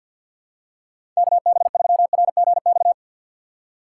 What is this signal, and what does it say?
Morse code, heard as a keyed tone, about 53 wpm, 700 Hz. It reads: K B 3 R G X
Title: USA Call Signs - 6 Characters - 6x Character Spacing - 50wpm